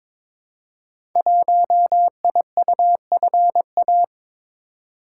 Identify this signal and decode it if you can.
Morse code, heard as a keyed tone, about 22 wpm, 700 Hz. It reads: 1IUFA
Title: Code Groups - 5 Letter:Numbers - 22wpm